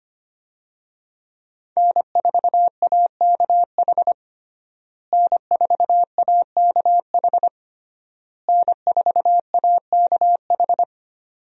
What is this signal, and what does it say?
Morse code, heard as a keyed tone, about 25 wpm, 700 Hz. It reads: N4AKH N4AKH N4AKH
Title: USA Call Signs - 5 Characters - Repeated 3x - 25wpm